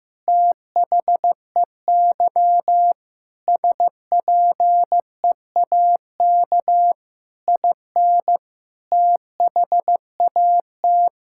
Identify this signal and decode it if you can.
Morse code, heard as a keyed tone, about 15 wpm, 700 Hz. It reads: THEY SPEAK IN THAT
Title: Sentences from Top 1600 Words 15wpm